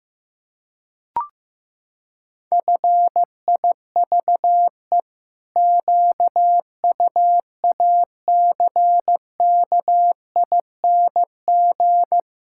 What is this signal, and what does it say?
Morse code, heard as a keyed tone, about 16 wpm, 700 Hz. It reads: FIVE QUACKING
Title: Pangram Challenge v2 - 15wpm